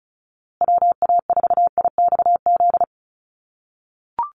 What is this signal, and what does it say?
Morse code, heard as a keyed tone, about 34 wpm, 700 Hz. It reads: WA4IXZ
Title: USA Call Signs Repeated 3x 35wpm